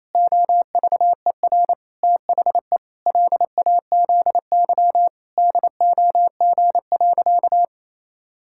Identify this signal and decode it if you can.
Morse code, heard as a keyed tone, about 28 wpm, 700 Hz. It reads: OVER THE LAZY DOG.
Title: Pangram Challenge v2 - 28wpm